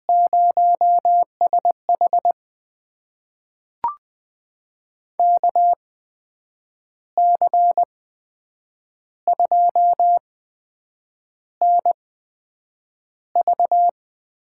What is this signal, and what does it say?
Morse code, heard as a keyed tone, about 20 wpm, spaced as for 7 wpm, 700 Hz. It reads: 0SH KC2NV
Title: USA Call Signs - 8x Character Spacing - 20wpm